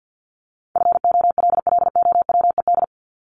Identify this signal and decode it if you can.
Morse code, heard as a keyed tone, about 50 wpm, 700 Hz. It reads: FOLLOWED
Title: ED Suffix - Top 100 Words with Base Words 50wpm